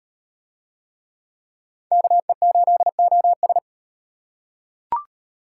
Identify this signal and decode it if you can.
Morse code, heard as a keyed tone, about 34 wpm, 700 Hz. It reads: KE8OS